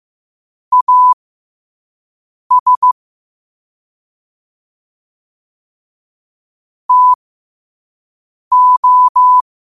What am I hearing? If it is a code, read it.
Morse code, heard as a keyed tone, about 15 wpm, spaced as for 5 wpm, 1000 Hz. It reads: AS TO
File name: qotdfiles / 5wpm